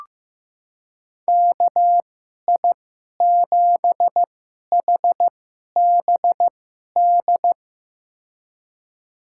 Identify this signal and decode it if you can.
Morse code, heard as a keyed tone, about 15 wpm, spaced as for 11 wpm, 700 Hz. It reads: KI7HBD
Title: USA Call Signs - 6 Characters - 2x Character Spacing - 15wpm